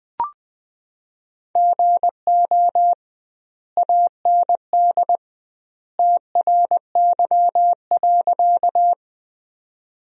Morse code, heard as a keyed tone, about 21 wpm, 700 Hz. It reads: GO AND TRY.
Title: 2x Word-Spacing - Sentences from Top 200 Words - 20wpm